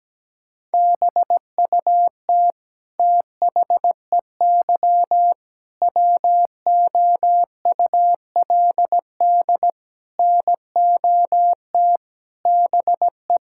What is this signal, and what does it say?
Morse code, heard as a keyed tone, about 17 wpm, 700 Hz. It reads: BUT THEY WOULD NOT BE
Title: Sentences from Top 1000 Words - Review Part 5 - 17wpm